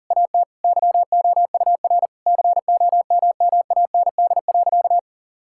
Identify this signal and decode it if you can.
Morse code, heard as a keyed tone, about 40 wpm, 700 Hz. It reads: AT YOUR COMMAND.
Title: Sentences from Top 1600 Words 40wpm